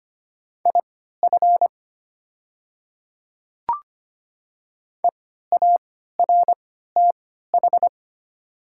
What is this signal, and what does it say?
Morse code, heard as a keyed tone, about 25 wpm, spaced as for 14 wpm, 700 Hz. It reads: IF EARTH
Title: 3x Character Spacing - Top 200 Words - 25wpm